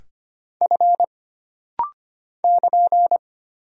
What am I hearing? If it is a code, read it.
Morse code, heard as a keyed tone, about 25 wpm, 700 Hz. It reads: F <KN>
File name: Single Character (Letter:Number:Punctuation:Prosign) - Rapid-Fire - 25wpm